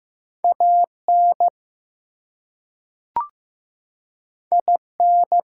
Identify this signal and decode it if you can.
Morse code, heard as a keyed tone, about 15 wpm, 700 Hz. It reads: AN IN
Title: Two Letter Words - 15wpm